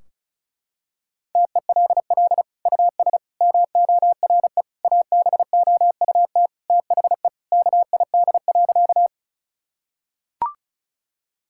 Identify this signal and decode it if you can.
Morse code, heard as a keyed tone, about 34 wpm, 700 Hz. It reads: TELL US MORE ABOUT THE KID.
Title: Sentences from Top 1600 Words 35wpm